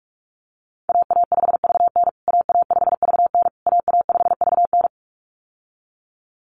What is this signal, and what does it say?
Morse code, heard as a keyed tone, about 44 wpm, 700 Hz. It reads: AA5VN AA5VN AA5VN
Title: USA Call Signs Repeated 3x 45wpm